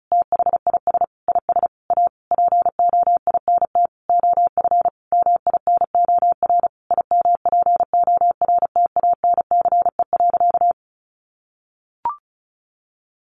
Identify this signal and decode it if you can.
Morse code, heard as a keyed tone, about 34 wpm, 700 Hz. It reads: THIS IS A POINT OF MINOR IMPORTANCE.
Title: Sentences from Top 1500 Words 35wpm